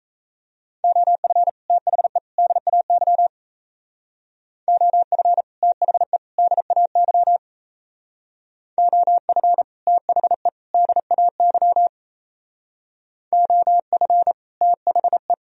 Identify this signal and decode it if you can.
Morse code, heard as a keyed tone, about 36 wpm, 700 Hz. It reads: OF THE DAY OF THE DAY OF THE DAY OF THE